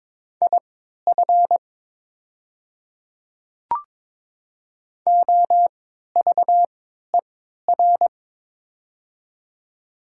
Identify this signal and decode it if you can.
Morse code, heard as a keyed tone, about 22 wpm, spaced as for 12 wpm, 700 Hz. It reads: IF OVER
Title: 3x Character Spacing - Top 200 Words - 22wpm